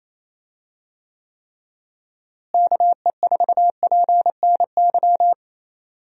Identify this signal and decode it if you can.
Morse code, heard as a keyed tone, about 28 wpm, 700 Hz. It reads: KE4PNY
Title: USA Call Signs - 6 Characters - Repeated 3x - 28wpm